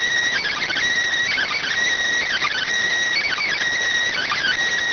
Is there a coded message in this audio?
Yes: Emergency Alert System (SAME) tones (header bursts) from 0.4 to 4.6 s, heard as harsh buzzing bursts.